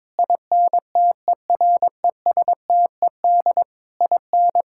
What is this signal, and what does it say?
Morse code, heard as a keyed tone, about 22 wpm, 700 Hz. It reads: INTERESTED IN
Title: Sentences from Top 1600 Words 22wpm